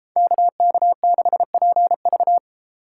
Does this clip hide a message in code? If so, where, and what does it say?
Yes: Morse code from 0.2 to 2.4 s, heard as a keyed tone, about 32 wpm, 700 Hz. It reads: KK6PV